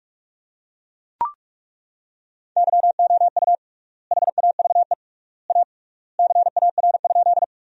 Morse code, heard as a keyed tone, about 42 wpm, 700 Hz. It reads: YOU HAVE A CAR?
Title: 3x Word-Spacing - Sentences from Top 300 Words - 45wpm